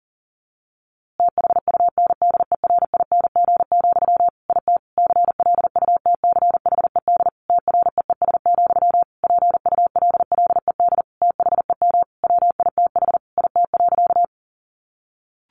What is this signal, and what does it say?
Morse code, heard as a keyed tone, about 40 wpm, 700 Hz. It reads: THUNDERING, IT CLUTCHED TREES, PULLED THEM WITH IT.
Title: A Sound Of Thunder 40wpm